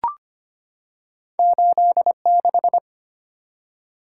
Morse code, heard as a keyed tone, about 25 wpm, 700 Hz. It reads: 86
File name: Two Letter-Number Combinations - 25wpm